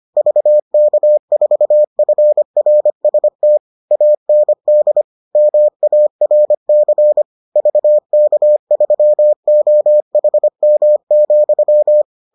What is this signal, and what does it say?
Morse code, heard as a keyed tone, about 25 wpm, 600 Hz. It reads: VK4FRST AND MARC VK3OHM,